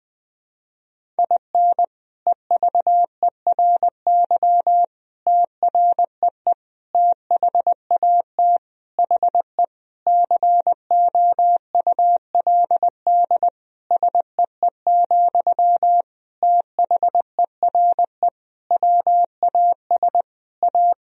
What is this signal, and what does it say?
Morse code, heard as a keyed tone, about 20 wpm, 700 Hz. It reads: IN EVERY TREE THAT HE COULD SEE, THERE WAS A
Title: Sentences from Top 1000 Words - Review Part 1 - 20wpm